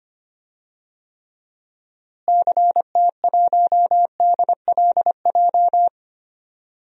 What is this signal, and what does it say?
Morse code, heard as a keyed tone, about 25 wpm, 700 Hz. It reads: CT1DLJ
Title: DXCC-100 Call Signs - 6 Characters - Repeated 3x - 25wpm